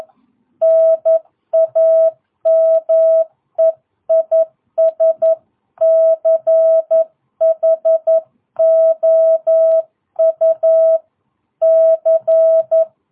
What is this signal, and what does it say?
Morse code, heard as a keyed tone, about 11 wpm, 650 Hz. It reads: NAME IS CHOU C
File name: jh8cbh / 2022cw